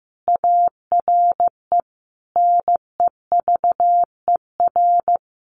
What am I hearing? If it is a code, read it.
Morse code, heard as a keyed tone, about 15 wpm, 700 Hz. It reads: ARE NEVER
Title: Sentences from Top 600 Words 15wpm